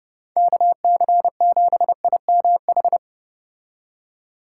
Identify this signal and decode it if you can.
Morse code, heard as a keyed tone, about 29 wpm, 700 Hz. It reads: KC7IMH